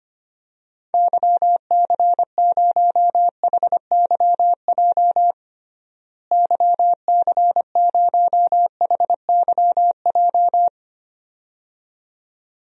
Morse code, heard as a keyed tone, about 25 wpm, 700 Hz. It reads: YC0HYJ YC0HYJ
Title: DXCC-100 Call Signs - 6 Characters - Repeated 2x - 25wpm